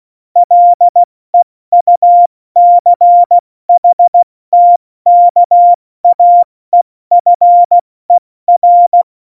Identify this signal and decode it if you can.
Morse code, heard as a keyed tone, about 16 wpm, spaced as for 14 wpm, 700 Hz. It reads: LEUCHTKAEFER
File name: Morsekurs / Witze / Witz 25. September 2018, 70 F80